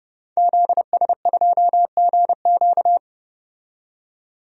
Morse code, heard as a keyed tone, about 30 wpm, 700 Hz. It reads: ZS2GQ
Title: DXCC-100 Call Signs - 5 Characters - 30wpm